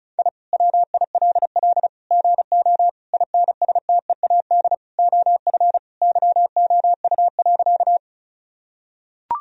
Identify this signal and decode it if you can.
Morse code, heard as a keyed tone, about 34 wpm, 700 Hz. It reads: I WILL GO INSTEAD OF YOU.
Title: Sentences from Top 1000 Words - Review Part 4 - 35wpm